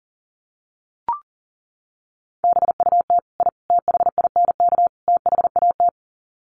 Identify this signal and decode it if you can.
Morse code, heard as a keyed tone, about 40 wpm, 700 Hz. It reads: BUT I THINK THAT
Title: Sets of 4 Words 40wpm